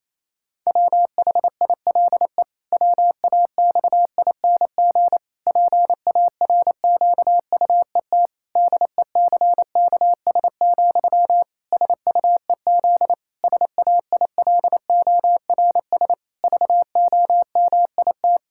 Morse code, heard as a keyed tone, about 28 wpm, 700 Hz. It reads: WHILE WAXING PARQUET DECKS, SUEZ SAILORS VOMIT
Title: Pangram Challenge v2 - 28wpm